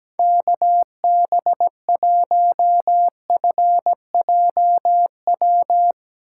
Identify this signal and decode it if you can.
Morse code, heard as a keyed tone, about 17 wpm, 700 Hz. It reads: KB1FJW
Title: USA Call Signs - 3x Character Spacing - 17wpm